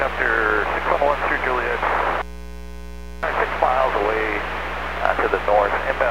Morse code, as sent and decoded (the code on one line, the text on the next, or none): none